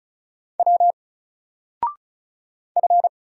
Morse code, heard as a keyed tone, about 33 wpm, 700 Hz. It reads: W F
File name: Single-Letter-Number-35wpm